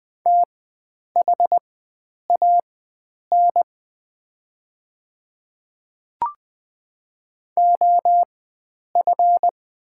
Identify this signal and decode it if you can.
Morse code, heard as a keyed tone, about 20 wpm, spaced as for 9 wpm, 690 Hz. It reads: THAN OF
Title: 4x Character Spacing - Top 100 Words - 20wpm